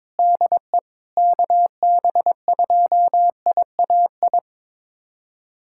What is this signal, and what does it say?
Morse code, heard as a keyed tone, about 21 wpm, 700 Hz. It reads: DE KB2IAI
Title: USA Callsign DE Callsign - 6 Characters - 22wpm